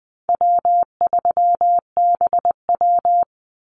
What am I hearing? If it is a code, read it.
Morse code, heard as a keyed tone, about 20 wpm, 700 Hz. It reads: W3BW
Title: Easy Call Signs - 20wpm